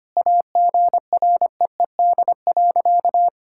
Morse code, heard as a keyed tone, about 25 wpm, 700 Hz. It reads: AGREED.
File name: practice / 25wpm / Sentences with 4-Word Phrases - 25wpm